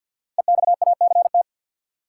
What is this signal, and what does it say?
Morse code, heard as a keyed tone, about 52 wpm, 700 Hz. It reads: EXACT
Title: Sentences from Top 1300 Words - New Word Drill - 50wpm